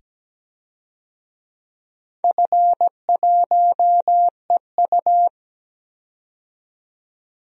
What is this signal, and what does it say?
Morse code, heard as a keyed tone, about 17 wpm, 700 Hz. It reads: F1EU
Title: DXCC-100 Call Signs - 4 Characters - 17wpm